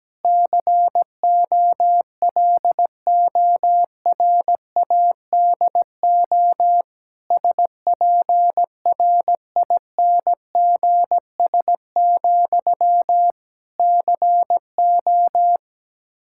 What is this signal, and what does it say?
Morse code, heard as a keyed tone, about 17 wpm, 700 Hz. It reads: COLORADO SPRINGS, CO
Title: Practice QSO - Exchange 1 - Beginning - 17wpm